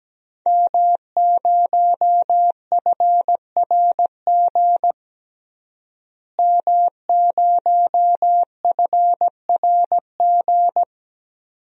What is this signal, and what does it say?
Morse code, heard as a keyed tone, about 17 wpm, 700 Hz. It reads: M0FRG M0FRG
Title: DXCC-100 Call Signs - 5 Characters - Repeated 2x - 17wpm